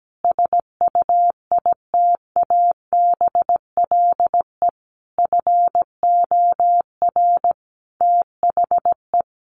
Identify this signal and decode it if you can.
Morse code, heard as a keyed tone, about 17 wpm, 700 Hz. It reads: SUITABLE FOR THE
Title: Sentences from Top 1400 Words 17wpm